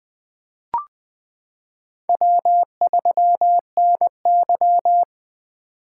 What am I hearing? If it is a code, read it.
Morse code, heard as a keyed tone, about 20 wpm, 700 Hz. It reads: W3NY
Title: 2023 POTA Activator Call Signs - 20wpm